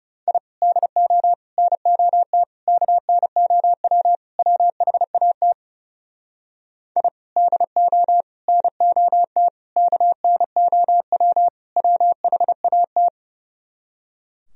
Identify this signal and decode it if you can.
Morse code, heard as a keyed tone, about 40 wpm, 700 Hz. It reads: I DO NOT KNOW WHAT I DO NOT KNOW WHAT